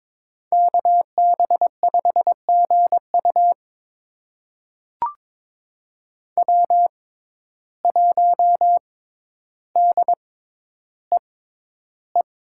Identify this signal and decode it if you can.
Morse code, heard as a keyed tone, about 22 wpm, spaced as for 11 wpm, 700 Hz. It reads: KB5GU W1DEE
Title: USA Call Signs - 5 Characters - 6x Character Spacing - 22wpm